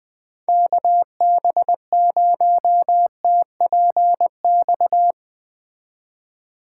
Morse code, heard as a keyed tone, about 20 wpm, 700 Hz. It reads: KB0TPX
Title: Speed Racing - USA 2x3 Call Signs 17wpm